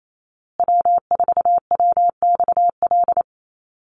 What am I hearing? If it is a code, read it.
Morse code, heard as a keyed tone, about 28 wpm, 700 Hz. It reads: W4WXL
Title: USA Call Signs Repeated 2x 28wpm